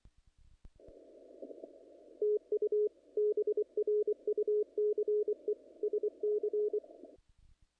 Morse code, heard as a keyed tone, about 24 wpm, 420 Hz. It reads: TU BRUCE SC
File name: contests / naqp-cw-k5tr-op-k5pi- -1800